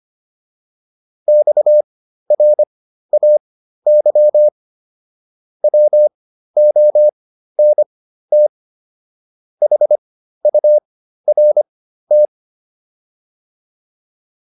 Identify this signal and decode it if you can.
Morse code, heard as a keyed tone, about 25 wpm, spaced as for 13 wpm, 600 Hz. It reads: XRAY WONT HURT